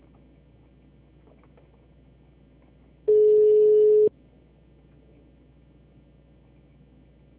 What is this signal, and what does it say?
Ringback tone (Europe, 425 Hz).
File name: out- -2007- - - .276